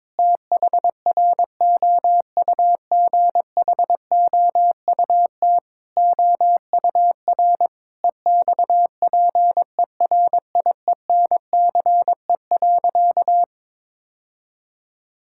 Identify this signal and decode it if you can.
Morse code, heard as a keyed tone, about 22 wpm, 700 Hz. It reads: THROUGHOUT OUR EXPERIENCE.
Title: Sentences from Top 1000 Words - Review Part 5 - 22wpm